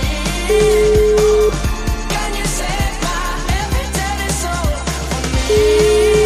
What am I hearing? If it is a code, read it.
Ringback tone (Europe, 425 Hz), with noise.